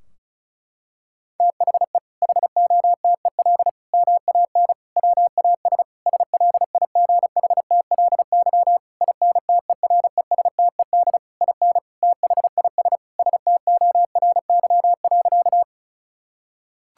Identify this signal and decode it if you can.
Morse code, heard as a keyed tone, about 36 wpm, 700 Hz. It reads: THE HOTEL MAN WAS SLIGHTLY INTERESTED IN THIS STORY.
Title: Sentences from Top 1600 Words 35wpm